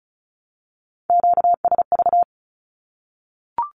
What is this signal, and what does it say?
Morse code, heard as a keyed tone, about 34 wpm, 700 Hz. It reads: QSV
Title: Q-Codes Repeated 3x 35wpm